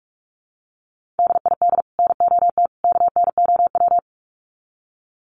Morse code, heard as a keyed tone, about 44 wpm, 700 Hz. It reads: DID NOT KNOW
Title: Sets of 3 Words 45wpm